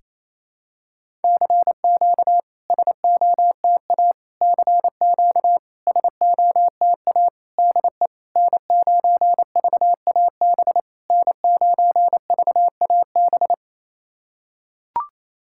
Morse code, heard as a keyed tone, about 28 wpm, 700 Hz. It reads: CQ SOTA CQ SOTA DE N9VAB N9VAB
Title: SOTA QSO Practice 28wpm